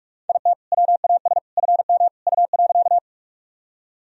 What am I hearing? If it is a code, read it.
Morse code, heard as a keyed tone, about 46 wpm, 700 Hz. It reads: IT WAS FM U.
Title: Sentences from Top 100 Words and Abbreviations 45wpm